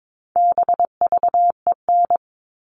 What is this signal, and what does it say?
Morse code, heard as a keyed tone, about 22 wpm, 700 Hz. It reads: BVEN
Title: All Cut Numbers - 4-Digit 22wpm